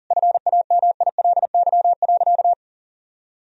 Morse code, heard as a keyed tone, about 40 wpm, 700 Hz. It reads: FAMILY.
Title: Sentences from Top 1000 Words - Review Part 4 - 40wpm